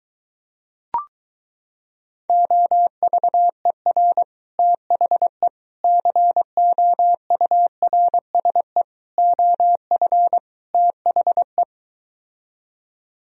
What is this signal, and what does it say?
Morse code, heard as a keyed tone, about 23 wpm, 700 Hz. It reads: OVER THE COURSE OF THE